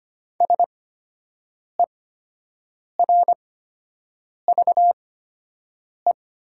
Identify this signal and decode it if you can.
Morse code, heard as a keyed tone, about 25 wpm, spaced as for 7 wpm, 700 Hz. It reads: SERVE
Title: 8x Character Spacing - Top 500 Words - 25wpm